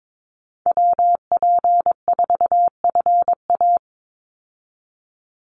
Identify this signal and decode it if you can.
Morse code, heard as a keyed tone, about 22 wpm, 700 Hz. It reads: WP4FA
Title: USA Call Signs 22wpm